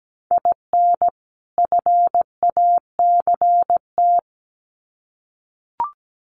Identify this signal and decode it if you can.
Morse code, heard as a keyed tone, about 17 wpm, 700 Hz. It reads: IN FACT
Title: Sets of 2 Words 17wpm